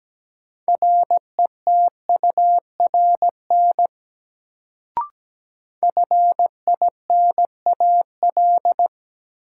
Morse code, heard as a keyed tone, about 17 wpm, 700 Hz. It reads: RETURN FINAL
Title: Sentences from Top 800 Words - New Word Drill - 17wpm